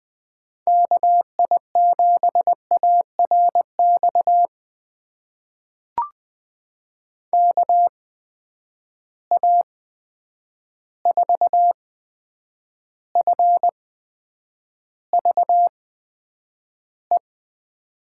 Morse code, heard as a keyed tone, about 20 wpm, spaced as for 9 wpm, 700 Hz. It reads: KI7ARX KA4FVE
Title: USA Call Signs - 6 Characters - 8x Character Spacing - 20wpm